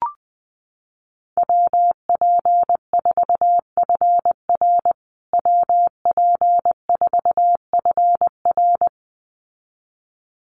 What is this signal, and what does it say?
Morse code, heard as a keyed tone, about 20 wpm, 700 Hz. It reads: WP4FR WP4FR
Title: USA Call Signs Repeated 2x 20wpm